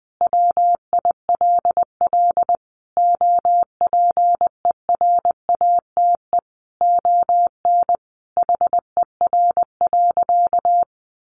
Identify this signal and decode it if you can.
Morse code, heard as a keyed tone, about 20 wpm, 700 Hz. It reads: WILL OPERATE ON HER.
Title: Sentences from Top 1500 Words 20wpm